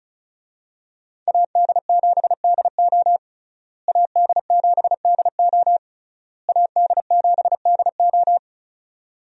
Morse code, heard as a keyed tone, about 36 wpm, 700 Hz. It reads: AD7DO AD7DO AD7DO
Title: USA Call Signs - 5 Characters - Repeated 3x - 35wpm